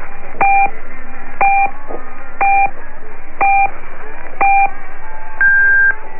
Radio time signal (time pips). It is heard from 0.4 to 5.9 s.
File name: contenido / estaciones-chinas / Chinese time signal and announcement 1300 UTC